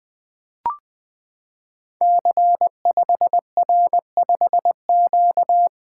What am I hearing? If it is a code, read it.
Morse code, heard as a keyed tone, about 21 wpm, 700 Hz. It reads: C5R5Q
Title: Code Groups - 5 Letter:Numbers - 20wpm